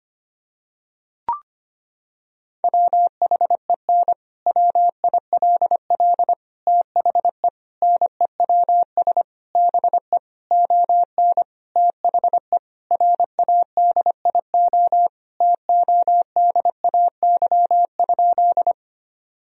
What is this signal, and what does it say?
Morse code, heard as a keyed tone, about 25 wpm, 700 Hz. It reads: WHEN WILL THE NEWS BE ON THE RADIO TODAY?
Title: Sentences from Top 1000 Words - Review Part 3 - 25wpm